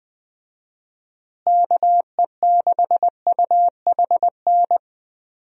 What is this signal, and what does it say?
Morse code, heard as a keyed tone, about 20 wpm, 700 Hz. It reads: KE6UHN
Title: USA Call Signs - 6 Characters - Repeated 3x - 20wpm